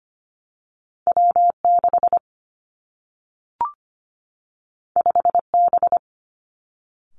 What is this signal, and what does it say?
Morse code, heard as a keyed tone, about 25 wpm, 700 Hz. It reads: W 6 5 B
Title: Two Challenging Letter-Number Combinations 25wpm